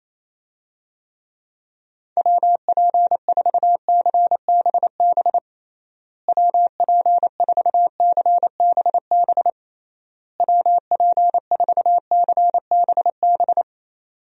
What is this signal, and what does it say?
Morse code, heard as a keyed tone, about 28 wpm, 700 Hz. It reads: WP4CBB WP4CBB WP4CBB
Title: USA Call Signs - 6 Characters - Repeated 3x - 28wpm